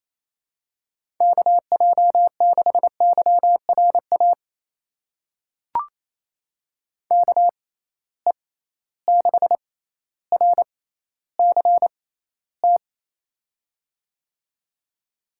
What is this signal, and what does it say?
Morse code, heard as a keyed tone, about 28 wpm, spaced as for 14 wpm, 700 Hz. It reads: KJ6YRA KE6RCT
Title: USA Call Signs - 6 Characters - 6x Character Spacing - 28wpm